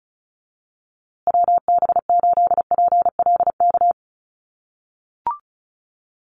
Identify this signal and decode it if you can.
Morse code, heard as a keyed tone, about 34 wpm, 700 Hz. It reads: WB8PLK
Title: USA Call Signs Repeated 2x 35wpm